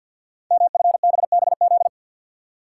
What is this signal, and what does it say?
Morse code, heard as a keyed tone, about 46 wpm, 700 Hz. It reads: MFBBZ
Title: Code Groups - 5 Letters - 50wpm